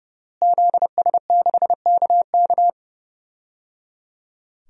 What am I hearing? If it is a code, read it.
Morse code, heard as a keyed tone, about 30 wpm, 700 Hz. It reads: ZS6KK
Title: DXCC-100 Call Signs - 5 Characters - Repeated 3x - 30wpm